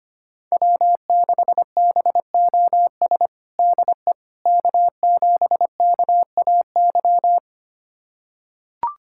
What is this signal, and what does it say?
Morse code, heard as a keyed tone, about 25 wpm, 700 Hz. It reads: W6BOS DE K7KAY
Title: USA Callsign DE Callsign - 5 Characters - 25wpm